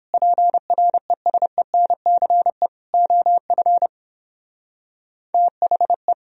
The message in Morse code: PRESENCE OF THE

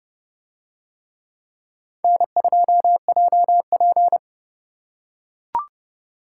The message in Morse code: N2JP

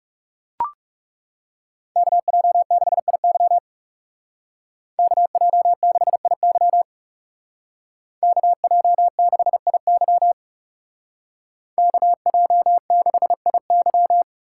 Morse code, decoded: KJ6IY KJ6IY KJ6IY KJ6IY

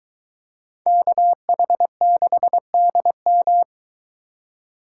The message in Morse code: KH6DM

